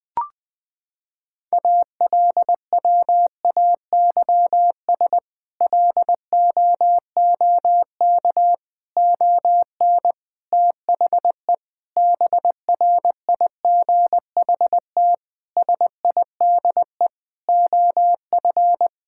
Morse code, heard as a keyed tone, about 20 wpm, 700 Hz. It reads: ALWAYS LOOK ON THE BRIGHT SIDE OF
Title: Sets of 8 Words - 20wpm